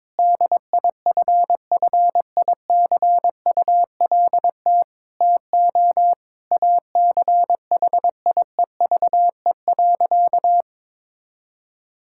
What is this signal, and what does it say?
Morse code, heard as a keyed tone, about 22 wpm, 700 Hz. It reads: DIFFICULT TO ACHIEVE.
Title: Sentences from Top 1000 Words - Review Part 6 - 22wpm